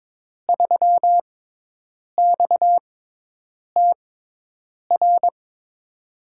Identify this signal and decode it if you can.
Morse code, heard as a keyed tone, about 22 wpm, spaced as for 8 wpm, 700 Hz. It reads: 3XTR
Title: USA Call Signs - 5 Characters - 6x Character Spacing - 22wpm